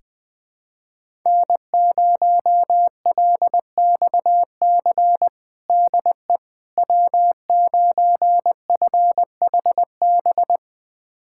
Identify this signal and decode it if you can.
Morse code, heard as a keyed tone, about 20 wpm, 700 Hz. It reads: N0LXC DE W9FHB
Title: USA Callsign DE Callsign - 5 Characters - 20wpm